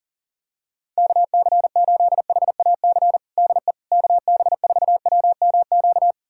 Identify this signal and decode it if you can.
Morse code, heard as a keyed tone, about 40 wpm, 700 Hz. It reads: KC8HAC DE KB4WMQ
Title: USA Callsign DE Callsign - 6 Characters - 40wpm